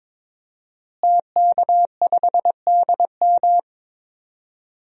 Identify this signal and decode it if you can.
Morse code, heard as a keyed tone, about 22 wpm, 700 Hz. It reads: TK5DM
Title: DXCC-100 Call Signs - 5 Characters - Repeated 2x - 22wpm